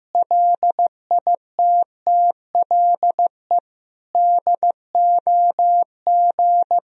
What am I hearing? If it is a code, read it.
Morse code, heard as a keyed tone, about 16 wpm, 700 Hz. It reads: LITTLE DOG
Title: Sentences from Top 500 Words and Abbreviations 15wpm